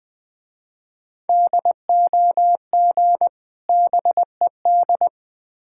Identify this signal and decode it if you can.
Morse code, heard as a keyed tone, about 20 wpm, 700 Hz. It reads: DOG BED
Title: Difficult Words with Hints - 20wpm